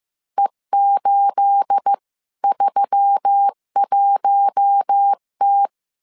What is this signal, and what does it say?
Morse code, heard as a keyed tone, about 15 wpm, 800 Hz. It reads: E831T
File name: CW / DET / 1 9 1500C S B 14wpm 3302